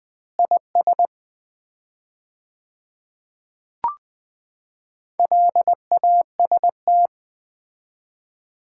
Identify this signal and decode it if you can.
Morse code, heard as a keyed tone, about 20 wpm, 690 Hz. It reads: IS LAST
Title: 4x Word-Spacing - Sets of 2 Words - 20wpm